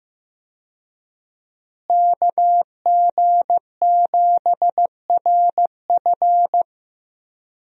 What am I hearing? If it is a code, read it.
Morse code, heard as a keyed tone, about 15 wpm, 700 Hz. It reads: KG7RF
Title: USA Call Signs - 5 Characters - Repeated 3x - 15wpm